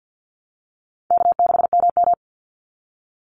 Morse code, heard as a keyed tone, about 50 wpm, 700 Hz. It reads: K6MM
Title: Call Signs - 50wpm